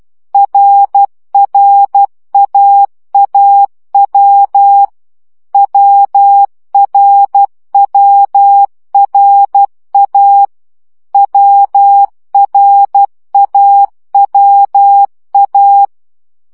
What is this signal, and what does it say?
Morse code, heard as a keyed tone, about 12 wpm, 800 Hz. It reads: RRAAW WRWRA WRAWA
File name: CW gyakorlas a r w 10 baud